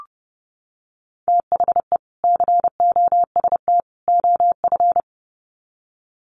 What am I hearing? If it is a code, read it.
Morse code, heard as a keyed tone, about 30 wpm, 700 Hz. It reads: THE COST OF
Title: Sets of 3 Words 30wpm